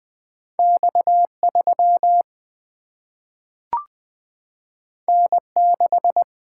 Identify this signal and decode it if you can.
Morse code, heard as a keyed tone, about 20 wpm, 700 Hz. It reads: X 3 N 6
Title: Single-Letter Single-Number 20wpm